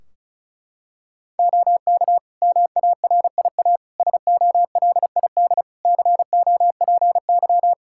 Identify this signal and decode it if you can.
Morse code, heard as a keyed tone, about 36 wpm, 700 Hz. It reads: OK MARIA SOLID COPY
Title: Practice QSO - Exchange 3 - More Info - 35wpm